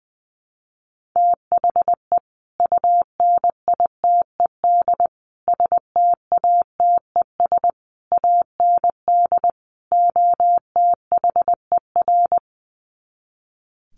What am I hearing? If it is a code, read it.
Morse code, heard as a keyed tone, about 20 wpm, 700 Hz. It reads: THE UNITED STATES AND OTHER
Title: Sets of 5 Words Encore 20wpm